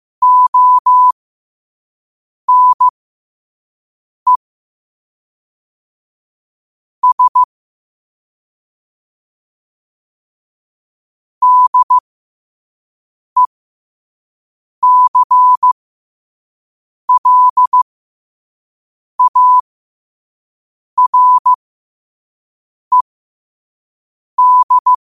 Morse code, heard as a keyed tone, about 15 wpm, spaced as for 5 wpm, 1000 Hz. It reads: ONE S DECLARED
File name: qotdfiles / 5wpm